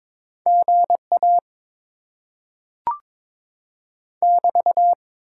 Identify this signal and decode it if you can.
Morse code, heard as a keyed tone, about 22 wpm, 700 Hz. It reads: GA <BT>
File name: Top 90 QSO Elements - 8x Character Spacing - 22wpm